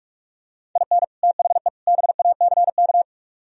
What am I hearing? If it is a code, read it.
Morse code, heard as a keyed tone, about 49 wpm, 700 Hz. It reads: IN THE BACK